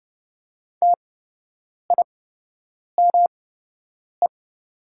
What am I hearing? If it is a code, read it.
Morse code, heard as a keyed tone, about 30 wpm, spaced as for 8 wpm, 700 Hz. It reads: TIME
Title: 8x Character Spacing - Top 100 Words - 30wpm